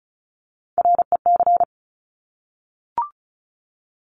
Morse code, heard as a keyed tone, about 36 wpm, 700 Hz. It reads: REC
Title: Top Three Letter Combinations in Words 35wpm